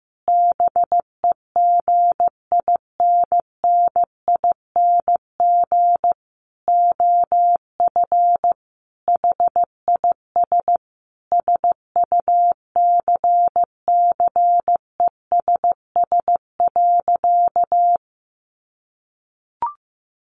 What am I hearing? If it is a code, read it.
Morse code, heard as a keyed tone, about 15 wpm, 700 Hz. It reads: BEGINNING OF HIS SUCCESS.
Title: Sentences from Top 900 Words 15wpm